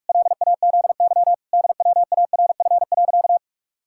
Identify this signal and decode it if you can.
Morse code, heard as a keyed tone, about 46 wpm, 700 Hz. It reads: LAZY DWARF.
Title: Pangram Challenge v2 - 45wpm